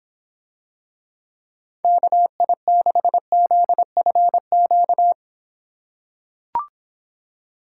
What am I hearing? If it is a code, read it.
Morse code, heard as a keyed tone, about 25 wpm, 700 Hz. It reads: KI6ZFQ